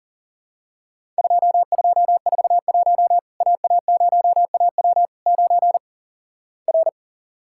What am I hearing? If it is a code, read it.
Morse code, heard as a keyed tone, about 37 wpm, 700 Hz. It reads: 2241 AA0AW 9 R
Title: Worked All Europe DX Contest - QTC Exchange - 40wpm